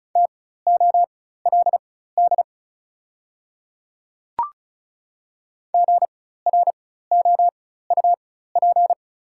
Morse code, heard as a keyed tone, about 36 wpm, spaced as for 16 wpm, 700 Hz. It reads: TOLD GROUP
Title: 4x Character Spacing - Top 500 Words - 35wpm